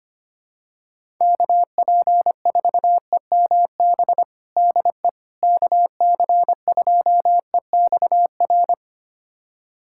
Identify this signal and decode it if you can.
Morse code, heard as a keyed tone, about 25 wpm, 700 Hz. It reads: KP4EMB DE KC2EXR
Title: USA Callsign DE Callsign 25wpm